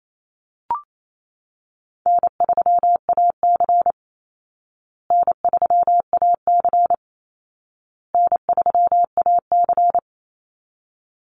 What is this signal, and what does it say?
Morse code, heard as a keyed tone, about 28 wpm, 700 Hz. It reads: N3AC N3AC N3AC
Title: Call Signs Repeated 2x - 28wpm